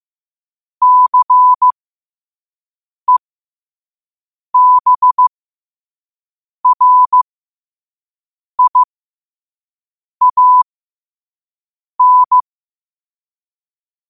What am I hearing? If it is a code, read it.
Morse code, heard as a keyed tone, about 15 wpm, spaced as for 5 wpm, 1000 Hz. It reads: CEBRIAN
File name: qotdfiles / 5wpm